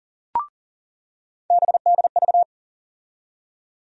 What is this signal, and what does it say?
Morse code, heard as a keyed tone, about 40 wpm, 700 Hz. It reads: BDV
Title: USA Call Sign Postfixes - 3 Letters 40wpm